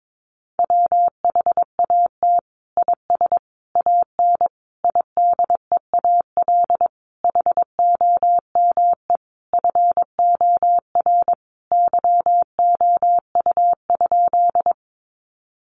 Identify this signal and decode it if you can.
Morse code, heard as a keyed tone, about 22 wpm, 700 Hz. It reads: WHAT IS AN IDEAL HOME FOR YOU?
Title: Sentences from Top 1400 Words 22wpm